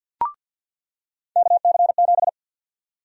Morse code, heard as a keyed tone, about 48 wpm, 700 Hz. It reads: KC7
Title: USA Call Sign Prefixes - 50wpm